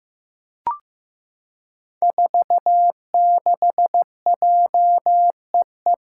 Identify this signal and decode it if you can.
Morse code, heard as a keyed tone, about 16 wpm, 700 Hz. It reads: 46JEE
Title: Five Challenging Letter-Number Combinations 15wpm